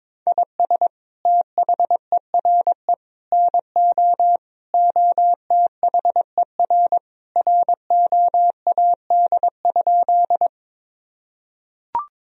Morse code, heard as a keyed tone, about 22 wpm, 700 Hz. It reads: IS THERE NO OTHER ROAD?
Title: Sentences from Top 1000 Words - Review Part 2 - 22wpm